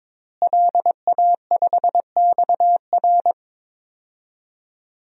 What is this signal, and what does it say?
Morse code, heard as a keyed tone, about 22 wpm, 700 Hz. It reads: LA5XR
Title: DXCC-100 Call Signs - 5 Characters - Repeated 3x - 22wpm